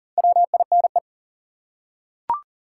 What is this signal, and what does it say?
Morse code, heard as a keyed tone, about 40 wpm, 700 Hz. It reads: WINE
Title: Four Letter Words - 40wpm